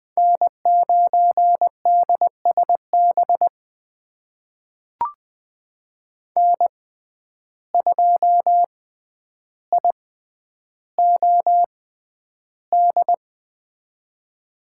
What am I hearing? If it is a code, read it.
Morse code, heard as a keyed tone, about 20 wpm, spaced as for 10 wpm, 700 Hz. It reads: N9DSB N2IOD
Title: USA Call Signs - 5 Characters - 6x Character Spacing - 20wpm